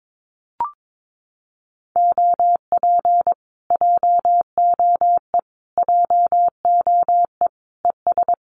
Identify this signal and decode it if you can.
Morse code, heard as a keyed tone, about 22 wpm, 700 Hz. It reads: OP JOE JOE ES
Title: Common QSO Phrases Encore 22wpm